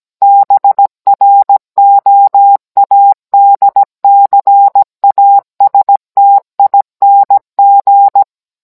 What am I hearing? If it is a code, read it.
Morse code, heard as a keyed tone, about 17 wpm, 800 Hz. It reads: BROADCASTING